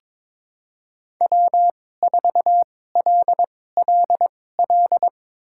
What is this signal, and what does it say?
Morse code, heard as a keyed tone, about 21 wpm, spaced as for 16 wpm, 700 Hz. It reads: W4LLL
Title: USA Call Signs - 5 Characters - 2x Character Spacing - 22wpm